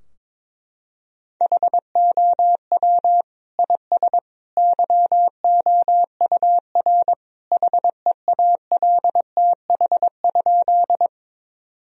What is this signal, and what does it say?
Morse code, heard as a keyed tone, about 22 wpm, 700 Hz. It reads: HOW IS YOUR HEALTH?
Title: Sentences from Top 1000 Words - Review Part 3 - 22wpm